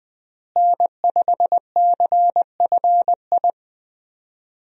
Morse code, heard as a keyed tone, about 20 wpm, 700 Hz. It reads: N5CFI
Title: USA Call Signs - 5 Characters - Repeated 3x - 20wpm